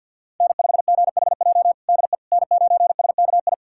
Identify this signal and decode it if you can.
Morse code, heard as a keyed tone, about 51 wpm, 700 Hz. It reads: N5GHJ DE N9SKI
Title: USA Callsign DE Callsign - 5 Characters - 50wpm